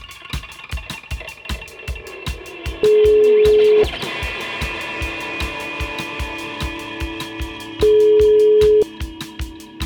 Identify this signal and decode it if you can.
Ringback tone (Europe, 425 Hz).